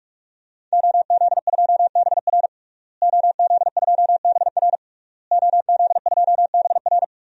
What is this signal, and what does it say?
Morse code, heard as a keyed tone, about 46 wpm, 700 Hz. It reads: OZ2BR OZ2BR OZ2BR
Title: DXCC-100 Call Signs - 5 Characters - Repeated 3x - 45wpm